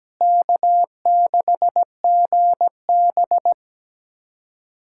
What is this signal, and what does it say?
Morse code, heard as a keyed tone, about 17 wpm, 700 Hz. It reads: K6GB
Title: USA Callsign DE Callsign - 4 Characters - 17wpm